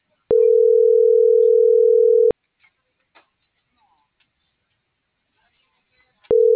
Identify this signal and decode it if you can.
Ringback tone (North America), 440 + 480 Hz.